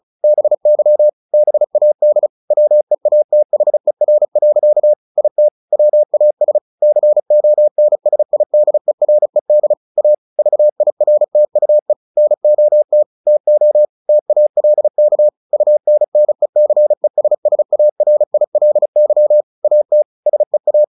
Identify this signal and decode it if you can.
Morse code, heard as a keyed tone, about 34 wpm, 600 Hz. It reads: BY BAD WEATHER. IT WAS CONSIDERED A VIRTUE NOT TO TALK UNNECESSARILY AT SEA
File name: the old man and the sea / the old man and the sea 1 35wpm